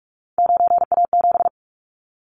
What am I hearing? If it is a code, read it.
Morse code, heard as a keyed tone, about 46 wpm, 690 Hz. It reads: 9A7
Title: Callsign Prefixes for Top 50 DXCC - 45wpm